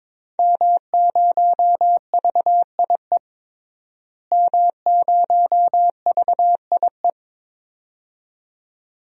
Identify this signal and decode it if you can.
Morse code, heard as a keyed tone, about 21 wpm, 700 Hz. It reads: M0VIE M0VIE
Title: DXCC-100 Call Signs - 5 Characters - Repeated 2x - 22wpm